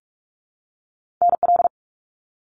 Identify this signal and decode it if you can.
Morse code, heard as a keyed tone, about 46 wpm, 700 Hz. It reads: NL
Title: Canadian Provinces and Territories 45wpm